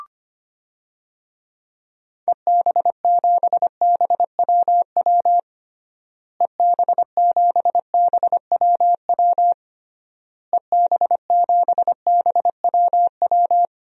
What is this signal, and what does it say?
Morse code, heard as a keyed tone, about 25 wpm, 700 Hz. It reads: EB7BWW EB7BWW EB7BWW
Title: DXCC-100 Call Signs - 6 Characters - Repeated 3x - 25wpm